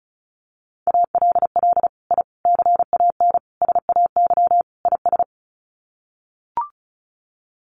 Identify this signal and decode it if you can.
Morse code, heard as a keyed tone, about 34 wpm, 700 Hz. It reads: ALL I CAN SAY IS
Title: Sets of 5 Words Encore2 35wpm